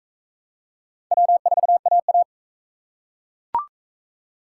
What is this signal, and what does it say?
Morse code, heard as a keyed tone, about 37 wpm, 700 Hz. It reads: W4AA